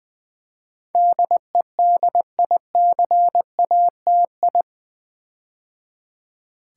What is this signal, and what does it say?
Morse code, heard as a keyed tone, about 20 wpm, 700 Hz. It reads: DEDICATI
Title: Build A Word - 4 Letters At A Time - 10-Letter Words - 20wpm